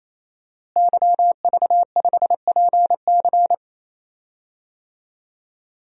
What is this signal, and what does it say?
Morse code, heard as a keyed tone, about 27 wpm, 700 Hz. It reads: YV5PC
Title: DXCC-100 Call Signs - 5 Characters - Repeated 3x - 28wpm